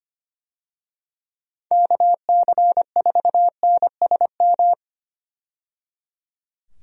Morse code, heard as a keyed tone, about 25 wpm, 700 Hz. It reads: KC4NSM
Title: USA Call Signs - 6 Characters - 25wpm